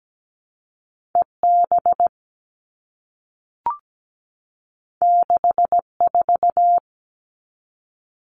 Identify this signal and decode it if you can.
Morse code, heard as a keyed tone, about 17 wpm, 700 Hz. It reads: EB 64
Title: All Cut Numbers - 2-Digit 17wpm